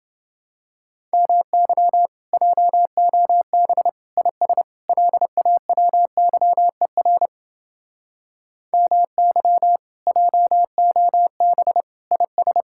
Morse code, heard as a keyed tone, about 28 wpm, 700 Hz. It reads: MY JOB IS LAWYER MY JOB IS